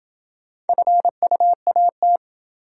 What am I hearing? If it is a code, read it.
Morse code, heard as a keyed tone, about 26 wpm, 700 Hz. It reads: FUAT